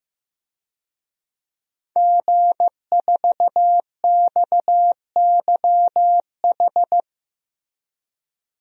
Morse code, heard as a keyed tone, about 15 wpm, 700 Hz. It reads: G4XYH
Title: DXCC-100 Call Signs - 5 Characters - 15wpm